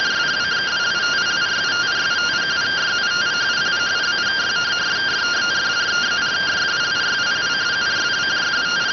Two-tone teleprinter signal, about 170 Hz shift.